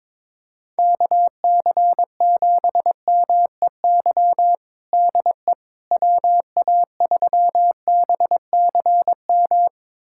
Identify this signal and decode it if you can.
Morse code, heard as a keyed tone, about 22 wpm, 700 Hz. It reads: KC7MEY DE WA3BCM
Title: USA Callsign DE Callsign 22wpm